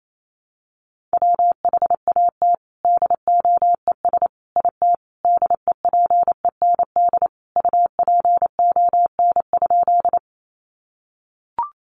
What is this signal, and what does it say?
Morse code, heard as a keyed tone, about 28 wpm, 700 Hz. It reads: WHAT DOES IT DEPEND UPON?
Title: Sentences from Top 1300 Words 28wpm